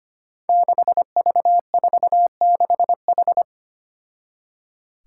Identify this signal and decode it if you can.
Morse code, heard as a keyed tone, about 25 wpm, 700 Hz. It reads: 6V46H
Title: Five Challenging Letter-Number Combinations 25wpm